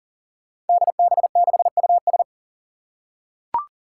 Morse code, heard as a keyed tone, about 40 wpm, 700 Hz. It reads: DB6US
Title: Five Challenging Letter-Number Combinations 40wpm